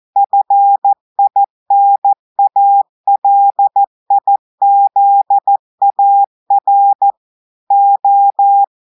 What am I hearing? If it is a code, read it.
Morse code, heard as a keyed tone, about 14 wpm, 800 Hz. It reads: FINALIZAR O
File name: audios / portugais / portugais texte en clair 2 vitesse 14 mots minutes